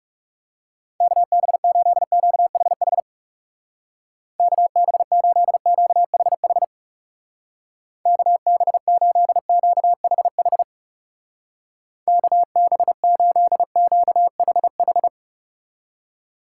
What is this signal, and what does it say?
Morse code, heard as a keyed tone, about 36 wpm, 700 Hz. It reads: KB8QHH KB8QHH KB8QHH KB8QHH
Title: Speed Racing - USA 2x3 Call Signs 30wpm